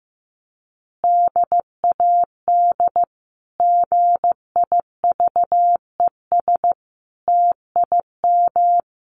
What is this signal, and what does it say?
Morse code, heard as a keyed tone, about 15 wpm, 700 Hz. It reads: DAD GIVES TIM
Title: Easy Phrases 15wpm